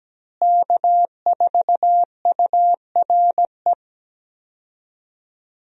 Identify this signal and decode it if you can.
Morse code, heard as a keyed tone, about 17 wpm, 700 Hz. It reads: K4URE
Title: USA Call Signs - 5 Characters - Repeated 3x - 17wpm